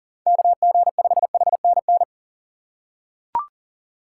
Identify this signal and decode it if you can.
Morse code, heard as a keyed tone, about 40 wpm, 700 Hz. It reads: KG5HNN
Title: USA Call Signs - 6 Characters - 3x Character Spacing - 40wpm